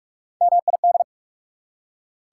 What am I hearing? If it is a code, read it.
Morse code, heard as a keyed tone, about 46 wpm, 690 Hz. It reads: MID
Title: Prefixes and Suffixes 45wpm